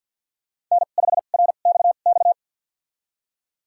Morse code, heard as a keyed tone, about 48 wpm, spaced as for 35 wpm, 700 Hz. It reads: N5RXX
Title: USA Call Signs - 5 Characters - 2x Character Spacing - 50wpm